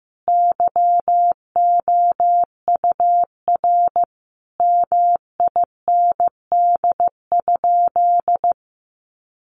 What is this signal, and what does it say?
Morse code, heard as a keyed tone, about 15 wpm, 700 Hz. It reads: YOUR MIND?
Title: Sentences from Top 1300 Words 15wpm